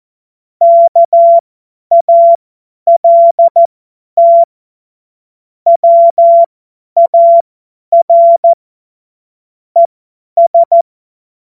Morse code, heard as a keyed tone, about 14 wpm, spaced as for 10 wpm, 680 Hz. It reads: KALT WAR ES
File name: Morsekurs / Witze / Witz 22. August 2023 50 F70